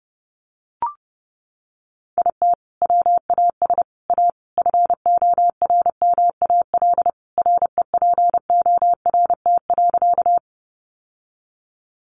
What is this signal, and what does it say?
Morse code, heard as a keyed tone, about 30 wpm, 700 Hz. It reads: IT WAS A FORMAL REPORT.
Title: Sentences from Top 1300 Words 30wpm